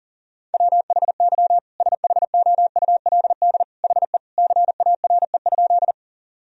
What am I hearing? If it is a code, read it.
Morse code, heard as a keyed tone, about 40 wpm, 700 Hz. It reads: WHY SHOULD HE CARE?
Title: Sentences from Top 1000 Words - Review Part 1 - 40wpm